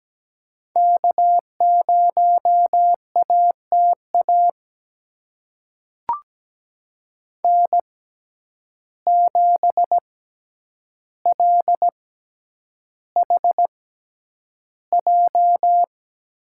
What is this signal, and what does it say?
Morse code, heard as a keyed tone, about 17 wpm, spaced as for 9 wpm, 700 Hz. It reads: K0ATA N7LHJ
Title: USA Call Signs - 5 Characters - 6x Character Spacing - 17wpm